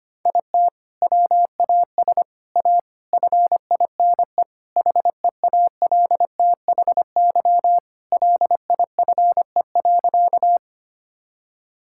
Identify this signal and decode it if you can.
Morse code, heard as a keyed tone, about 25 wpm, 700 Hz. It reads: IT WAS A FINE HEALTHY LIFE.
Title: Sentences from Top 1000 Words - Review Part 5 - 25wpm